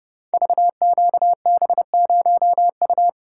Morse code, heard as a keyed tone, about 30 wpm, 700 Hz. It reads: VQB0U
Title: Code Groups - 5 Letter:Numbers - 30wpm